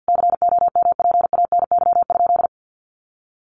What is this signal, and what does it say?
Morse code, heard as a keyed tone, about 48 wpm, 700 Hz. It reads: COMPANY?
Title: Sentences from Top 600 Words 50wpm